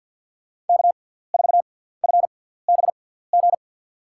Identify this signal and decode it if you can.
Morse code, heard as a keyed tone, about 51 wpm, spaced as for 17 wpm, 700 Hz. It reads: K4FBG
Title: USA Call Signs - 5 Characters - 6x Character Spacing - 50wpm